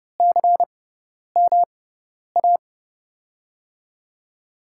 Morse code, heard as a keyed tone, about 30 wpm, spaced as for 10 wpm, 700 Hz. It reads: CMA
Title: USA Call Signs - 5 Characters - 6x Character Spacing - 30wpm